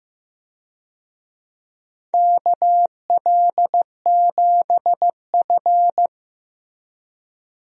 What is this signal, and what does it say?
Morse code, heard as a keyed tone, about 15 wpm, 700 Hz. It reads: KL7F
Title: DXCC-100 Call Signs - 4 Characters - Repeated 3x - 15wpm